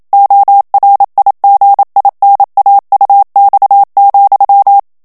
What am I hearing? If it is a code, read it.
Morse code, heard as a keyed tone, about 27 wpm, 800 Hz. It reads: ORIGINAUX,
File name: FRANCAIS TEXTE CLAIR 2 VITESSE 26 MOTS MINUTES